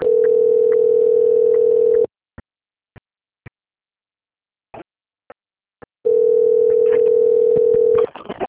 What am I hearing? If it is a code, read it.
Ringback tone (North America), 440 + 480 Hz.
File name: rg-1600-unknown- - - .38172